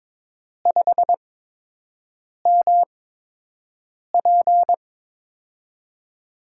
Morse code, heard as a keyed tone, about 22 wpm, spaced as for 6 wpm, 700 Hz. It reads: 5MP
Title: USA Call Signs - 8x Character Spacing - 22wpm